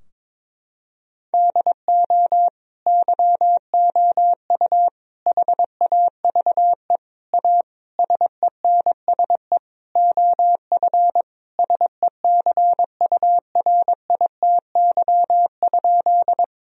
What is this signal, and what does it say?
Morse code, heard as a keyed tone, about 22 wpm, 700 Hz. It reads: DO YOU HAVE A SENSE OF SECURITY?
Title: Sentences from Top 1000 Words - Review Part 4 - 22wpm